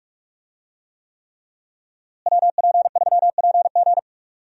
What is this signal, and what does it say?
Morse code, heard as a keyed tone, about 46 wpm, 700 Hz. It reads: WP3PG
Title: USA Call Signs - 5 Characters - Repeated 3x - 45wpm